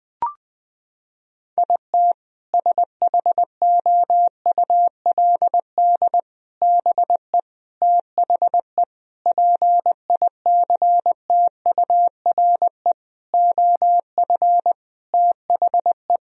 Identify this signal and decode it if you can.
Morse code, heard as a keyed tone, about 20 wpm, 700 Hz. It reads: IT SHOULD BE THE PICTURE OF THE
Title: Sentences from Top 1000 Words - Review Part 1 - 20wpm